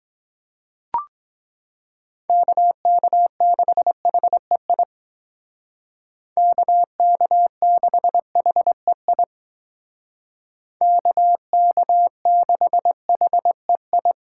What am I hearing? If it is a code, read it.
Morse code, heard as a keyed tone, about 20 wpm, 700 Hz. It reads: KK6HEI KK6HEI KK6HEI
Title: Speed Racing - USA Call Signs 17wpm